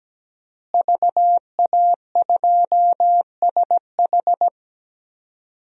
Morse code, heard as a keyed tone, about 17 wpm, 700 Hz. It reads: VA2SH
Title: DXCC-100 Call Signs - 5 Characters - Repeated 3x - 17wpm